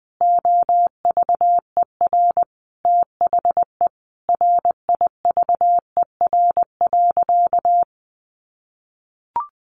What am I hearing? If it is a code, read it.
Morse code, heard as a keyed tone, about 20 wpm, 700 Hz. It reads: OVER THE RIVER.
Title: Sentences from Top 1300 Words 20wpm